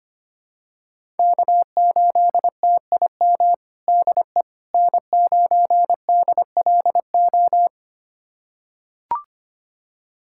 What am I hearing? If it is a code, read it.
Morse code, heard as a keyed tone, about 25 wpm, 700 Hz. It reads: K8TIM DE N9DLO
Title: USA Callsign DE Callsign - 5 Characters - 25wpm